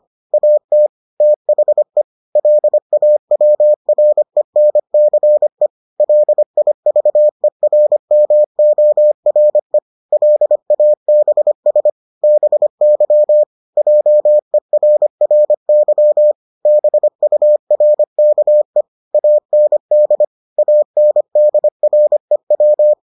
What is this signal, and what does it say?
Morse code, heard as a keyed tone, about 25 wpm, 600 Hz. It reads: AT THE LAWRENCE LIVERMORE LABS BY JERRY BURKE AND ANDREW